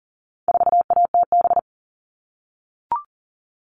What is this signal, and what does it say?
Morse code, heard as a keyed tone, about 40 wpm, 700 Hz. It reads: 4ATB
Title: All Cut Numbers - 4-Digit 40wpm